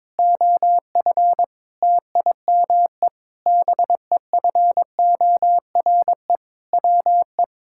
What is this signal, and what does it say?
Morse code, heard as a keyed tone, about 22 wpm, 700 Hz. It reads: OF TIME BEFORE WE
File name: practice / 22wpm / Sentences with 8-Word Phrases - 22wpm